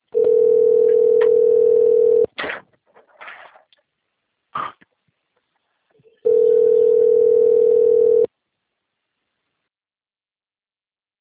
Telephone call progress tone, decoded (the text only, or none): ringback tone